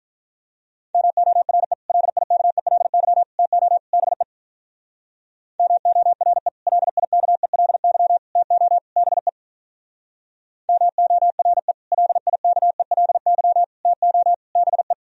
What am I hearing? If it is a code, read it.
Morse code, heard as a keyed tone, about 45 wpm, 700 Hz. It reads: MORE LIKELY TO BE MORE LIKELY TO BE MORE LIKELY TO BE